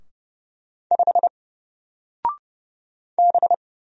Morse code, heard as a keyed tone, about 30 wpm, 700 Hz. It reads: 5 B
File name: Single-Letter-Number-30wpm